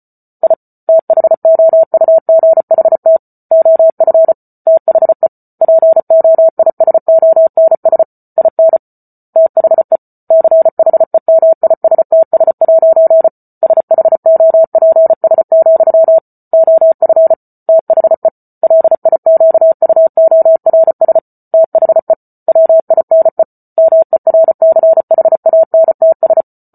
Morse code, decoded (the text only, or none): I THOUGHT OF THE POISONS IN THE CHEMISTS' SHOPS, OF THE LIQUORS THE WINE MERCHANTS